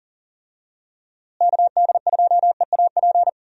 Morse code, heard as a keyed tone, about 39 wpm, 700 Hz. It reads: KD2EAP
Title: USA Call Signs - 6 Characters - Repeated 2x - 40wpm